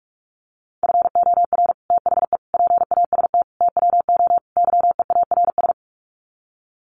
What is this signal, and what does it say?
Morse code, heard as a keyed tone, about 44 wpm, 700 Hz. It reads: FOR THE PAST TWO YEARS
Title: Sets of 5 Words Encore2 45wpm